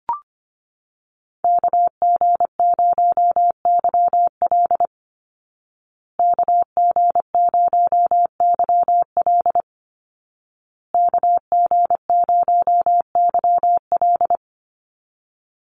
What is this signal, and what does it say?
Morse code, heard as a keyed tone, about 25 wpm, 700 Hz. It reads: KG0YL KG0YL KG0YL
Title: Call Signs Repeated 2x - 25wpm